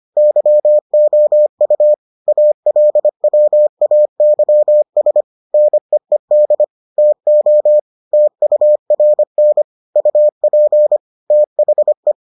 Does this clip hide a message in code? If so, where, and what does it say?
Yes: Morse code from 0.2 to 12.1 s, heard as a keyed tone, about 25 wpm, 600 Hz. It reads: YOU ALWAYS NEED TO TURN UP THE